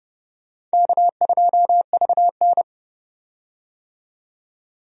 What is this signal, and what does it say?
Morse code, heard as a keyed tone, about 31 wpm, 700 Hz. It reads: K2VN
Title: USA Call Signs - 4 Characters - 30wpm